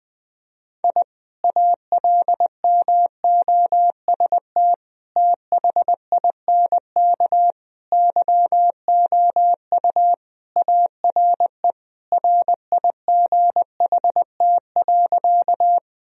Morse code, heard as a keyed tone, about 20 wpm, 700 Hz. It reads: I ALMOST THINK YOU ARE RIGHT.
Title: Sentences from Top 1000 Words - Review Part 4 - 20wpm